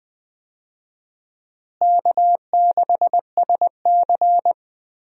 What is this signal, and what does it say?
Morse code, heard as a keyed tone, about 20 wpm, 700 Hz. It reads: K6SC
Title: USA Call Signs - 4 Characters - Repeated 2x - 20wpm